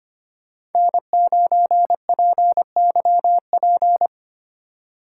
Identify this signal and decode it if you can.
Morse code, heard as a keyed tone, about 32 wpm, 700 Hz. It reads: N9PYP